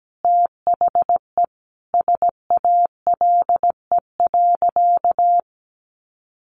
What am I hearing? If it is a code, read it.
Morse code, heard as a keyed tone, about 17 wpm, 700 Hz. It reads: THE SALE.
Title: Sentences from Top 1400 Words 17wpm